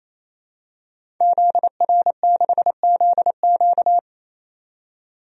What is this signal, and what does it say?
Morse code, heard as a keyed tone, about 28 wpm, 700 Hz. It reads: ZR6ZQ
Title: DXCC-100 Call Signs - 5 Characters - 28wpm